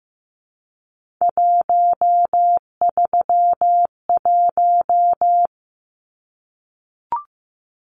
Morse code, heard as a keyed tone, about 15 wpm, 700 Hz. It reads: 131
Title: Numbers - 3-Digit 15wpm